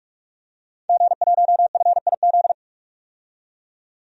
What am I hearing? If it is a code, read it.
Morse code, heard as a keyed tone, about 41 wpm, 700 Hz. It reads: G1FIZ
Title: DXCC-100 Call Signs - 5 Characters - 45wpm